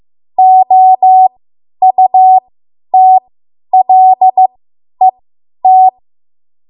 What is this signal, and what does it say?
Morse code, heard as a keyed tone, about 15 wpm, spaced as for 10 wpm, 750 Hz. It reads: OUTLET
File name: Archive / 10 WPM / 10WPM